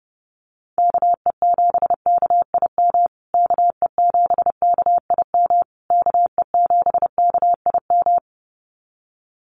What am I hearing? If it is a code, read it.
Morse code, heard as a keyed tone, about 30 wpm, 700 Hz. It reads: KE7KIM KE7KIM KE7KIM
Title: USA Call Signs Repeated 3x 30wpm